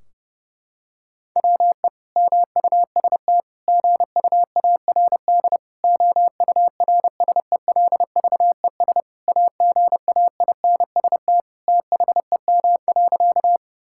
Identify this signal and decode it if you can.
Morse code, heard as a keyed tone, about 30 wpm, 700 Hz. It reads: WE MUST GUARD OURSELVES AGAINST THEM.
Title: Sentences from Top 1600 Words 30wpm